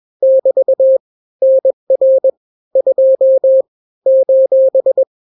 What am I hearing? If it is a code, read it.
Morse code, heard as a keyed tone, about 21 wpm, 530 Hz. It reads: <BT> NR 2 :
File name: daily / 20wpm